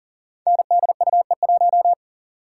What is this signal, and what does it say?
Morse code, heard as a keyed tone, about 38 wpm, 700 Hz. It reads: NDUE1
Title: Code Groups - 5 Letter:Numbers - 40wpm